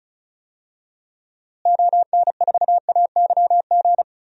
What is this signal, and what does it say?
Morse code, heard as a keyed tone, about 36 wpm, 700 Hz. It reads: ON4AYG
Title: DXCC-100 Call Signs - 6 Characters - Repeated 2x - 35wpm